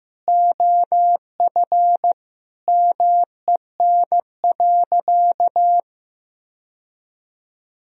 Morse code, heard as a keyed tone, about 15 wpm, 700 Hz. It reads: OF MEN.
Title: Sentences from Top 1000 Words - Review Part 3 - 15wpm